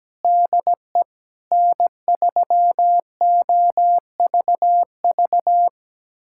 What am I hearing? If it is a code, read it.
Morse code, heard as a keyed tone, about 17 wpm, 700 Hz. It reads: DE N3OVV
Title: USA Callsign DE Callsign - 5 Characters - 17wpm